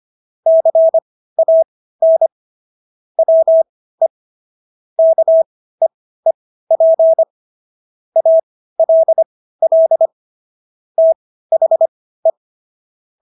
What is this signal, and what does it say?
Morse code, heard as a keyed tone, about 25 wpm, spaced as for 15 wpm, 650 Hz. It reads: CAN WE KEEP ALL THE